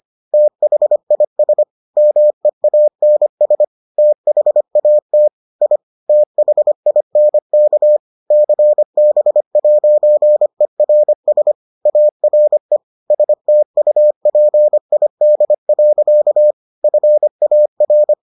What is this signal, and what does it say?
Morse code, heard as a keyed tone, about 25 wpm, 600 Hz. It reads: THIS MEANS THAT I THINK CB'ERS ARE STUPID. FAR